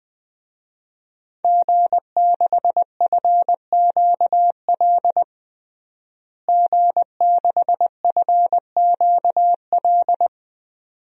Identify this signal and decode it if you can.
Morse code, heard as a keyed tone, about 20 wpm, 700 Hz. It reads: G6FQL G6FQL
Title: DXCC-100 Call Signs - 5 Characters - Repeated 2x - 20wpm